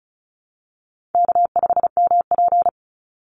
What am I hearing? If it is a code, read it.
Morse code, heard as a keyed tone, about 34 wpm, 700 Hz. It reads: K5MP
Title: Easy Call Signs Repeated 3x - 35wpm